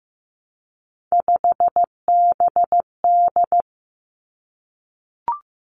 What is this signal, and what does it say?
Morse code, heard as a keyed tone, about 15 wpm, 700 Hz. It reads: 5BD
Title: Three Challenging Letter-Number Combinations 15wpm